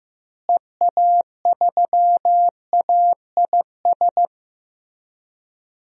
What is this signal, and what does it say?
Morse code, heard as a keyed tone, about 15 wpm, 700 Hz. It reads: EA3AIS
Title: DXCC-100 Call Signs - 6 Characters - Repeated 2x - 15wpm